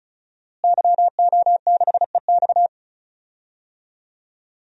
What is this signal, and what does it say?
Morse code, heard as a keyed tone, about 36 wpm, 700 Hz. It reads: YO6EX
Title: DXCC-100 Call Signs - 5 Characters - Repeated 3x - 35wpm